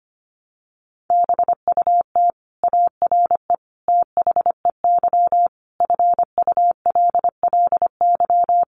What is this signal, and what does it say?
Morse code, heard as a keyed tone, about 25 wpm, 700 Hz. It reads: BUT ARE THEY FULLY
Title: Sentences from Top 1300 Words 25wpm